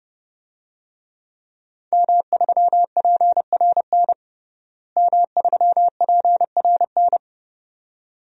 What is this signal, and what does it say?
Morse code, heard as a keyed tone, about 30 wpm, 700 Hz. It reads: M3PRN M3PRN
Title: DXCC-100 Call Signs - 5 Characters - Repeated 2x - 30wpm